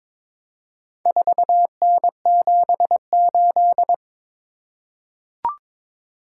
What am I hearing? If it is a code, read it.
Morse code, heard as a keyed tone, about 22 wpm, 700 Hz. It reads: 4N78
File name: practice / 22wpm / Four Letter-Number Combinations - 22wpm